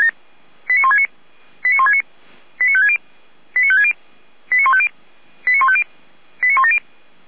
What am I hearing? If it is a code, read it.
Five-tone selective call (ZVEI), heard as quick runs of five tones, tones of about 70 ms. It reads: 89169 89169 89560 89560 89150 89150 89170